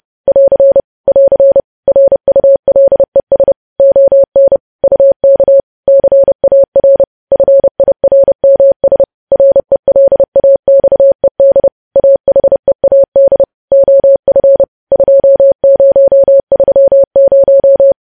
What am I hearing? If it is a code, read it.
Morse code, heard as a keyed tone, about 30 wpm, 550 Hz. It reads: <AR> <AR> RULES ON UK CAR FIRMS RELAXED AHEAD OF 2030
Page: news headlines in morse code at 30 wpm